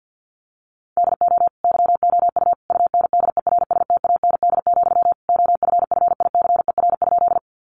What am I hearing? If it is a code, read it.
Morse code, heard as a keyed tone, about 49 wpm, 700 Hz. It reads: DO YOU UNDERSTAND, OFFICER?
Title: Sentences from Top 1200 Words 50wpm